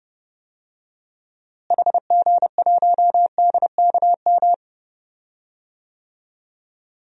Morse code, heard as a keyed tone, about 30 wpm, 700 Hz. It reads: HG1DKM
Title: DXCC-100 Call Signs - 6 Characters - Repeated 3x - 30wpm